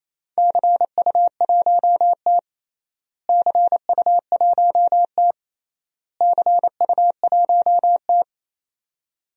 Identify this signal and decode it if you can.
Morse code, heard as a keyed tone, about 27 wpm, 700 Hz. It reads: CU1T CU1T CU1T
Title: DXCC-100 Call Signs - 4 Characters - Repeated 3x - 28wpm